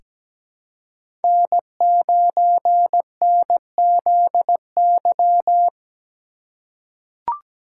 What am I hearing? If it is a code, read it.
Morse code, heard as a keyed tone, about 17 wpm, 700 Hz. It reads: N9NZY
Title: USA Call Signs - 5 Characters - 3x Character Spacing - 17wpm